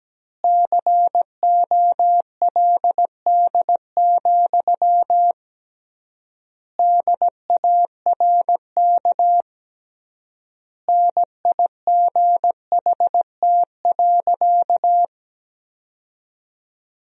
Morse code, heard as a keyed tone, about 17 wpm, 700 Hz. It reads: COLD, DARK NIGHT.
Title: 3x Word-Spacing - Sentences from Top 500 Words - 17wpm